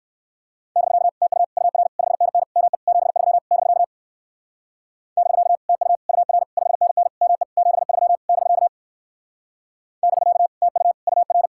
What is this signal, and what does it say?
Morse code, heard as a keyed tone, about 62 wpm, 700 Hz. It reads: <BK> TU UR 5NN ME 73 <BK> <BK> TU UR 5NN ME 73 <BK> <BK> TU UR